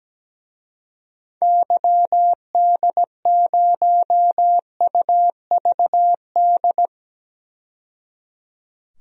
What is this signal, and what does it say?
Morse code, heard as a keyed tone, about 17 wpm, 700 Hz. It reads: YD0UVD
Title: DXCC-100 Call Signs - 6 Characters - 17wpm